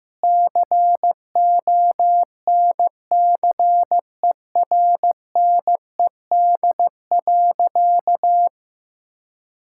Morse code, heard as a keyed tone, about 15 wpm, 700 Hz. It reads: CONCERNED.
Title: Sentences from Top 1000 Words - Review Part 6 - 15wpm